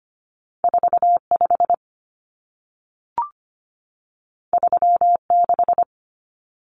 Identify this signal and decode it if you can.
Morse code, heard as a keyed tone, about 25 wpm, 700 Hz. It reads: 45 36
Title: Common Cut Numbers - 2-Digit 25wpm